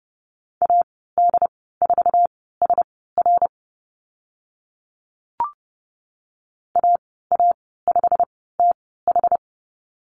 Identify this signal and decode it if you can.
Morse code, heard as a keyed tone, about 30 wpm, spaced as for 17 wpm, 700 Hz. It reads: AD4SR AA5TH
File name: Lesson 15g - Number 5 - Call Signs 30wpm